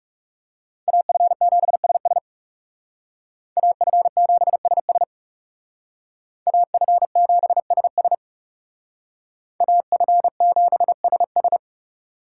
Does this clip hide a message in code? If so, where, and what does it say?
Yes: Morse code from 0.9 to 11.6 s, heard as a keyed tone, about 38 wpm, 700 Hz. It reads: AF7SS AF7SS AF7SS AF7SS